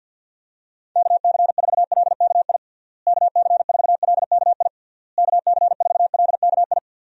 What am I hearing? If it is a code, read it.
Morse code, heard as a keyed tone, about 48 wpm, 700 Hz. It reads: KC4LKI KC4LKI KC4LKI
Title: USA Call Signs - 6 Characters - Repeated 3x - 50wpm